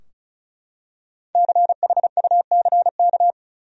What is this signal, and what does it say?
Morse code, heard as a keyed tone, about 36 wpm, 700 Hz. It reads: CHUCK
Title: USA Common Names - Max 5 Letters - 35wpm